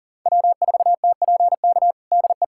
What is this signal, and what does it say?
Morse code, heard as a keyed tone, about 40 wpm, 700 Hz. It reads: W4TPK DE
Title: USA Callsign DE Callsign - 5 Characters - 40wpm